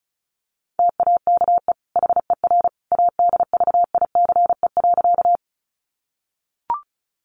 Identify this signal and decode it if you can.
Morse code, heard as a keyed tone, about 34 wpm, 700 Hz. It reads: TAKE HER ADVICE.
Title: Sentences from Top 1100 Words 35wpm